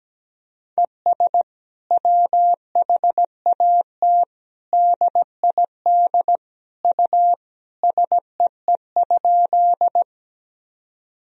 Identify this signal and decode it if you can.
Morse code, heard as a keyed tone, about 17 wpm, 700 Hz. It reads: ES WHAT DID U SEE?
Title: Sentences from Top 100 Words and Abbreviations 17wpm